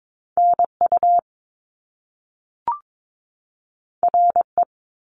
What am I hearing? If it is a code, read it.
Morse code, heard as a keyed tone, about 22 wpm, 700 Hz. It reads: NU RE
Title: Top Two Letter Combinations in Words 22wpm